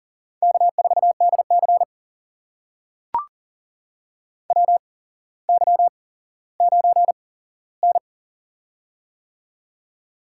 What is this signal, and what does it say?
Morse code, heard as a keyed tone, about 40 wpm, spaced as for 17 wpm, 700 Hz. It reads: K4DC WY9N
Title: USA Call Signs - 8x Character Spacing - 40wpm